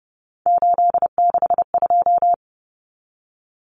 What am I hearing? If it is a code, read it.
Morse code, heard as a keyed tone, about 30 wpm, 700 Hz. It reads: 862
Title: Common Cut Numbers - 3-Digit 30wpm